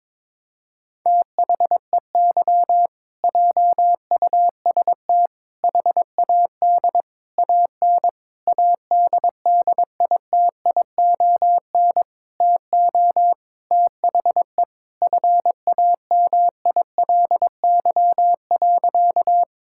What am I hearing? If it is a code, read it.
Morse code, heard as a keyed tone, about 22 wpm, 700 Hz. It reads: THEY JUST HAD AN ADDITION TO THE FAMILY.
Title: Sentences from Top 1000 Words - Review Part 6 - 22wpm